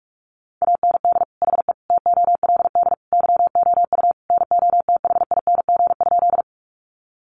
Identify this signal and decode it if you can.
Morse code, heard as a keyed tone, about 44 wpm, 700 Hz. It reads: AND HE TOLD YOU NOTHING?
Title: Sentences from Top 500 Words 45wpm